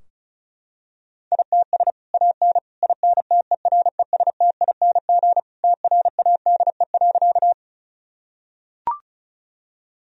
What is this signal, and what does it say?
Morse code, heard as a keyed tone, about 36 wpm, 700 Hz. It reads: ITS AN INTERESTING TRADE.
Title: Sentences from Top 1000 Words - Review Part 4 - 35wpm